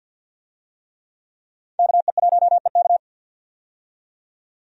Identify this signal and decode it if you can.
Morse code, heard as a keyed tone, about 52 wpm, 700 Hz. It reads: KE1EK
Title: USA Call Signs - 5 Characters - 50wpm